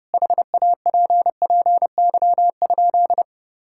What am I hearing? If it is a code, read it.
Morse code, heard as a keyed tone, about 32 wpm, 700 Hz. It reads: HAPPY?
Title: Sentences from Top 1000 Words - Review Part 6 - 30wpm